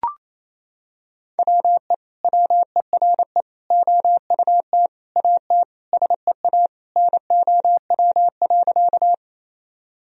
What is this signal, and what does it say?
Morse code, heard as a keyed tone, about 28 wpm, 700 Hz. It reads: WE WERE OUT AT SEA NOW.
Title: Sentences from Top 1000 Words - Review Part 1 - 28wpm